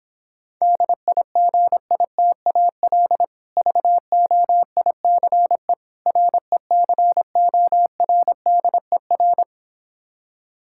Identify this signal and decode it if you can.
Morse code, heard as a keyed tone, about 26 wpm, 700 Hz. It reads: DIGITAL VOICE RECORDER